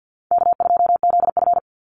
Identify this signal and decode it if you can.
Morse code, heard as a keyed tone, about 50 wpm, 700 Hz. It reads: K2ZF
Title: Easy Call Signs - 50wpm